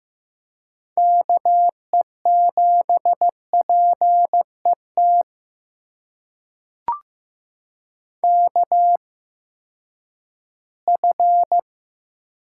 Morse code, heard as a keyed tone, about 15 wpm, spaced as for 10 wpm, 700 Hz. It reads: KE7PET KF
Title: USA Call Signs - 6 Characters - 8x Character Spacing - 15wpm